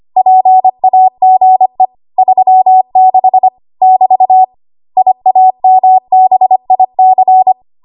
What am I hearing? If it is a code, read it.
Morse code, heard as a keyed tone, about 25 wpm, 750 Hz. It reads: PAGE 36 <BT> IAMBIC
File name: Archive / 25 WPM / 25WPM